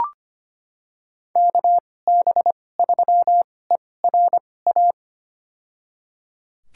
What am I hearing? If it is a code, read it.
Morse code, heard as a keyed tone, about 25 wpm, spaced as for 18 wpm, 700 Hz. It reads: KB3ERA
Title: USA Call Signs - 2x Character Spacing - 25wpm